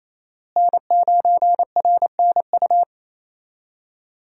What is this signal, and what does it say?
Morse code, heard as a keyed tone, about 28 wpm, 700 Hz. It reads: N9RNU
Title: USA Call Signs - 5 Characters - Repeated 2x - 28wpm